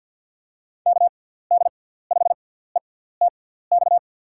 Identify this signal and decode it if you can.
Morse code, heard as a keyed tone, about 52 wpm, spaced as for 17 wpm, 700 Hz. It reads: KD5ETX
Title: USA Call Signs - 6 Characters - 6x Character Spacing - 50wpm